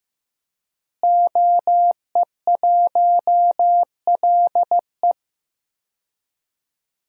Morse code, heard as a keyed tone, about 15 wpm, 700 Hz. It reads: OE1LE
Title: DXCC-100 Call Signs - 5 Characters - Repeated 3x - 15wpm